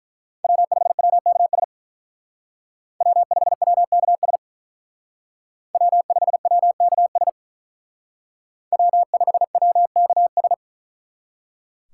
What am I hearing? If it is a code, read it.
Morse code, heard as a keyed tone, about 41 wpm, 700 Hz. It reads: W5WKS W5WKS W5WKS W5WKS
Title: Speed Racing - USA 1x3 Call Signs 35wpm